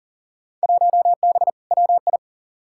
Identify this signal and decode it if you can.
Morse code, heard as a keyed tone, about 37 wpm, 700 Hz. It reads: 1B WI
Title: Field Day Exchanges - 40wpm